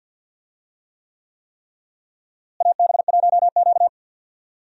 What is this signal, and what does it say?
Morse code, heard as a keyed tone, about 51 wpm, 700 Hz. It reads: AB1Q
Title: USA Call Signs - 4 Characters - Repeated 3x - 50wpm